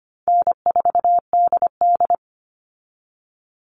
Morse code, heard as a keyed tone, about 25 wpm, 700 Hz. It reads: N4DD
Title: Call Signs Repeated 2x - 25wpm